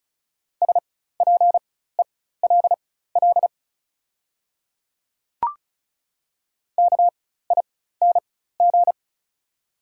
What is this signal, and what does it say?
Morse code, heard as a keyed tone, about 36 wpm, spaced as for 16 wpm, 690 Hz. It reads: SPELL KING
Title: 4x Character Spacing - Top 500 Words - 35wpm